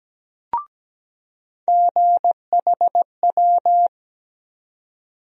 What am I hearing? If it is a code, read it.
Morse code, heard as a keyed tone, about 17 wpm, 700 Hz. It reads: GHW
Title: USA Call Sign Postfixes - 3 Letters 17wpm